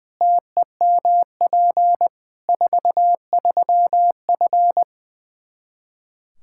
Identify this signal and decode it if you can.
Morse code, heard as a keyed tone, about 20 wpm, 700 Hz. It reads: TEMP 43F
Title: Practice QSO - Exchange 2 - Info - 20wpm